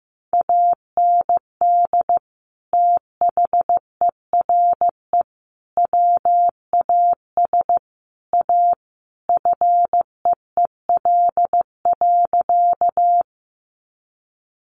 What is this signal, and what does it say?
Morse code, heard as a keyed tone, about 15 wpm, 700 Hz. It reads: AND THERE WAS A FEEL.
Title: A Sound Of Thunder 15wpm